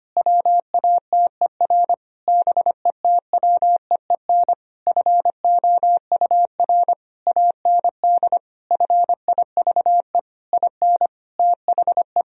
Morse code, heard as a keyed tone, about 25 wpm, 700 Hz. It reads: WATER BETWEEN FOUR AND FIVE IN THE
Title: Sentences from Top 1000 Words - Review Part 2 - 25wpm